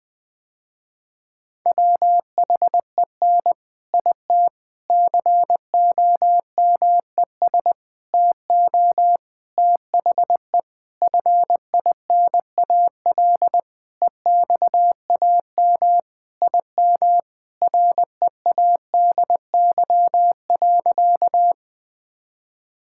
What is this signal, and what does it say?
Morse code, heard as a keyed tone, about 20 wpm, 700 Hz. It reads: WHEN IT COMES TO THE FINAL EXAM IM READY.
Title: Sentences with 5-Word Phrases - Spoken Beginning - 20wpm